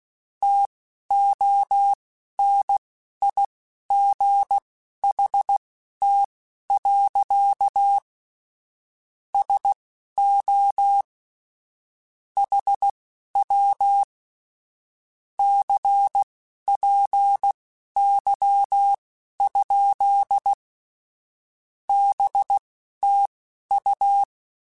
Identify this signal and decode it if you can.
Morse code, heard as a keyed tone, about 16 wpm, spaced as for 11 wpm, 800 Hz. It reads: TONIGHT. SO HW CPY? BTU
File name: QSOs / QSOs-mp3s / WA7QOX 1@10